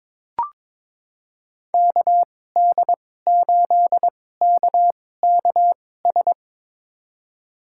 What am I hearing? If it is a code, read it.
Morse code, heard as a keyed tone, about 22 wpm, spaced as for 16 wpm, 700 Hz. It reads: KD8KKS
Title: USA Call Signs - 6 Characters - 2x Character Spacing - 22wpm